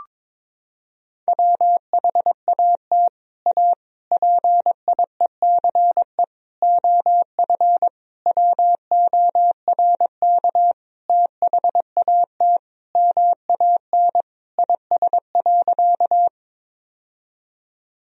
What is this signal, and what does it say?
Morse code, heard as a keyed tone, about 22 wpm, 700 Hz. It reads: WHAT A PIECE OF WORK THAT MAN IS.
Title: Sentences from Top 1000 Words - Review Part 2 - 22wpm